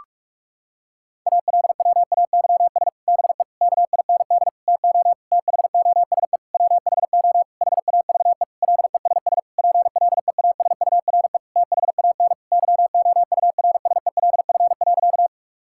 Morse code, heard as a keyed tone, about 46 wpm, 700 Hz. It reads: ALWAYS BE KIND TO THOSE WHO HAVE LESS PLEASURE THAN YOURSELF.
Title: Sentences from Top 1600 Words 45wpm